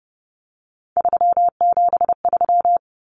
Morse code, heard as a keyed tone, about 30 wpm, 700 Hz. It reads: 373
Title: Numbers - 3-Digit 30wpm